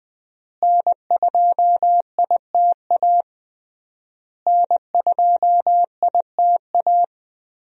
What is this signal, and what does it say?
Morse code, heard as a keyed tone, about 20 wpm, 700 Hz. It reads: N2ITA N2ITA
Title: USA Call Signs - 5 Characters - Repeated 3x - 20wpm